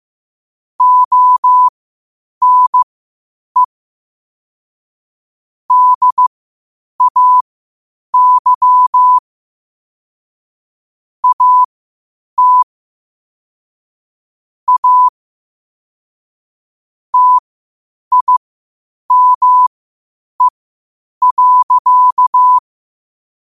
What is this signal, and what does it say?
Morse code, heard as a keyed tone, about 15 wpm, spaced as for 8 wpm, 1000 Hz. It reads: ONE DAY AT A TIME.
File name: qotdfiles / 8wpm